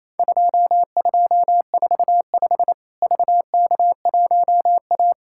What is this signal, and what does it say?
Morse code, heard as a keyed tone, about 27 wpm, 700 Hz. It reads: 2245 VK1A
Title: Worked All Europe DX Contest - QTC Exchange - 28wpm